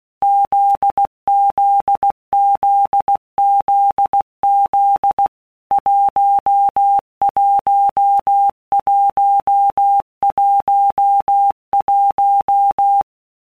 Morse code, heard as a keyed tone, about 16 wpm, 800 Hz. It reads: ZZZZZ 11111
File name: Basic Letter Groups By Blocks / Groups@15 / JZ17@15